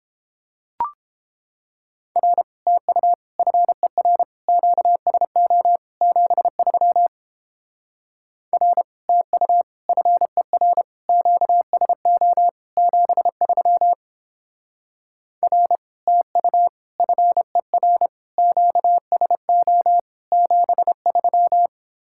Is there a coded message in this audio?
Yes: Morse code from 2.2 to 21.7 s, heard as a keyed tone, about 29 wpm, 700 Hz. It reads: R TU FER QSO 73 R TU FER QSO 73 R TU FER QSO 73